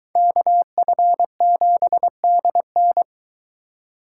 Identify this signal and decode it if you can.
Morse code, heard as a keyed tone, about 22 wpm, 700 Hz. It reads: KF7DN